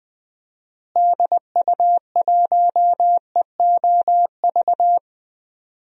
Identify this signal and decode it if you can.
Morse code, heard as a keyed tone, about 20 wpm, 700 Hz. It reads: DU1EOV
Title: DXCC-100 Call Signs - 6 Characters - Repeated 2x - 20wpm